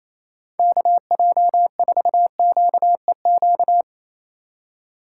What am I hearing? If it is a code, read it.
Morse code, heard as a keyed tone, about 34 wpm, 700 Hz. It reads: KJ4QEQ
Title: Speed Racing - USA 2x3 Call Signs 28wpm